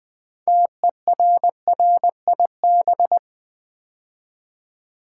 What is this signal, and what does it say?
Morse code, heard as a keyed tone, about 20 wpm, 700 Hz. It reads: TERRIB
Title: Build A Word - 3 Letters At A Time - 8-Letter Words - 20wpm